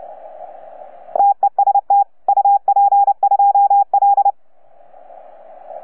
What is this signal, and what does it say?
Morse code, heard as a keyed tone, about 29 wpm, 810 Hz. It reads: TEST UP2L
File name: UP2L-20-CW